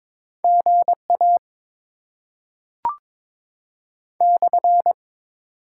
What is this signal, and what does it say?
Morse code, heard as a keyed tone, about 21 wpm, 700 Hz. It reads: GA /
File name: Top 90 QSO Elements - 8x Character Spacing - 22wpm